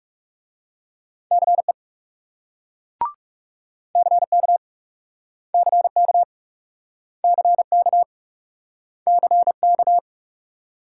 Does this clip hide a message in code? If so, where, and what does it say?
Yes: Morse code from 1.3 to 10.0 s, heard as a keyed tone, about 40 wpm, 700 Hz. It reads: CE CK CK CK CK